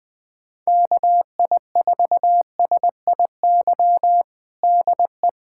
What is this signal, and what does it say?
Morse code, heard as a keyed tone, about 20 wpm, 700 Hz. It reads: KI4SIY DE
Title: USA Callsign DE Callsign 20wpm